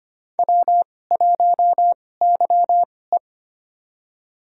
Morse code, heard as a keyed tone, about 25 wpm, spaced as for 18 wpm, 700 Hz. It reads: W1YE
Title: USA Call Signs - 4 Characters - 2x Character Spacing - 25wpm